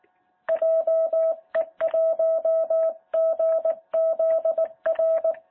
Morse code, heard as a keyed tone, about 18 wpm, 650 Hz. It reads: JE1GZR
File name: jh8cbh / 2021cw